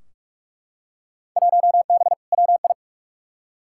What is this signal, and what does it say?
Morse code, heard as a keyed tone, about 44 wpm, 700 Hz. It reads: 1B WI
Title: Field Day Exchanges Encore - 45wpm